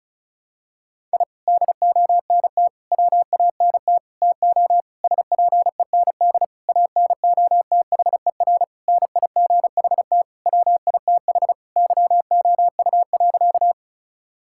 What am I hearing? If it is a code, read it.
Morse code, heard as a keyed tone, about 34 wpm, 690 Hz. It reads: I DONT WANT TO SPEND ANOTHER NIGHT WITH YOU.
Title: Sentences from Top 1000 Words - Review Part 5 - 35wpm